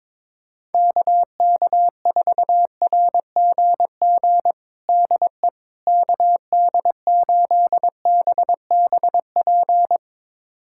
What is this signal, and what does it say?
Morse code, heard as a keyed tone, about 22 wpm, 700 Hz. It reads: KK4RGG DE KD8BBP
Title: USA Callsign DE Callsign 22wpm